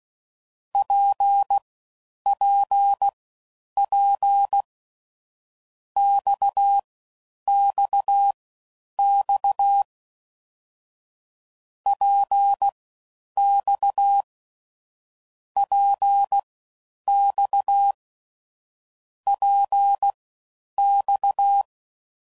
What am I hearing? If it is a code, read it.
Morse code, heard as a keyed tone, about 16 wpm, spaced as for 9 wpm, 800 Hz. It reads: PPP XXX PX PX PX